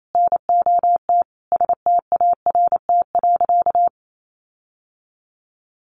Morse code, heard as a keyed tone, about 28 wpm, 700 Hz. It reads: NOT START.
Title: Sentences from Top 1500 Words 28wpm